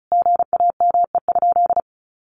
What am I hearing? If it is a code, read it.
Morse code, heard as a keyed tone, about 34 wpm, 700 Hz. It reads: GAME?
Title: Sentences from Top 1500 Words 35wpm